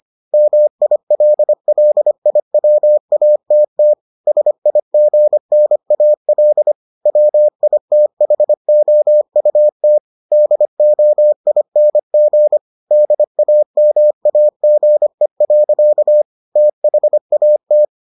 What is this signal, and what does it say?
Morse code, heard as a keyed tone, about 25 wpm, 600 Hz. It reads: MILLIWATT SIGNAL WITHOUT DOING DAMAGE. THAT